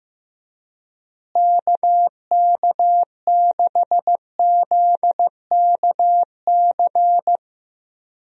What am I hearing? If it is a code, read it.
Morse code, heard as a keyed tone, about 15 wpm, 700 Hz. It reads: KK6ZKC
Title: USA Call Signs - 6 Characters - Repeated 3x - 15wpm